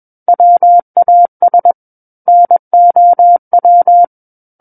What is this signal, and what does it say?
Morse code, heard as a keyed tone, about 21 wpm, 700 Hz. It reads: WAS NOW
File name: Book One / 12-Chapter 12 - 21 WPM